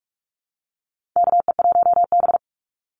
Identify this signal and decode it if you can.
Morse code, heard as a keyed tone, about 44 wpm, 700 Hz. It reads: KE1B
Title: Call Signs Repeated 2x - 45wpm